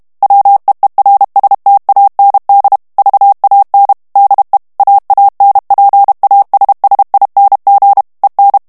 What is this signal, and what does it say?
Morse code, heard as a keyed tone, about 32 wpm, 800 Hz. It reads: WEERSTAND VAN DE AANPASSING EN
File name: morse mp3 / HOLLANDAIS texte clair 1 vitesse 30 mots minute